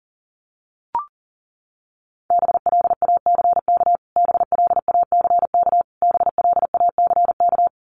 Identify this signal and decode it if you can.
Morse code, heard as a keyed tone, about 40 wpm, 700 Hz. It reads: BLACK BLACK BLACK
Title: Top 500 Words Repeated 40wpm